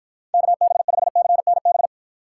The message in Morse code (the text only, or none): KB5CNB